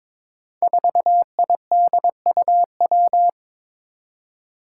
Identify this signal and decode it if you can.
Morse code, heard as a keyed tone, about 22 wpm, 700 Hz. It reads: 4IDUW
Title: Five Challenging Letter-Number Combinations 22wpm